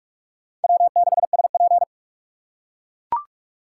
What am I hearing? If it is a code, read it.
Morse code, heard as a keyed tone, about 44 wpm, 700 Hz. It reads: W6SP